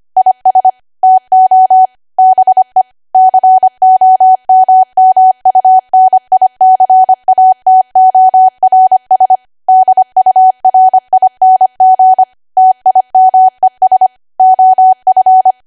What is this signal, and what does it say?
Morse code, heard as a keyed tone, about 25 wpm, 750 Hz. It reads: IS TO BE COMMUNICATORS DURING TIMES OF
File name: Archive / 25 WPM / 25WPM